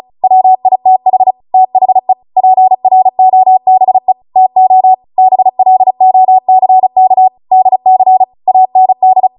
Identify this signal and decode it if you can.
Morse code, heard as a keyed tone, about 36 wpm, 750 Hz. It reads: WITH THE PROBE TO BLOCK DC AND